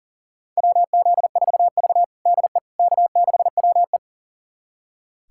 Morse code, heard as a keyed tone, about 40 wpm, 700 Hz. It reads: WZ4V DE K6WE
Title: USA Callsign DE Callsign - 4 Characters - 40wpm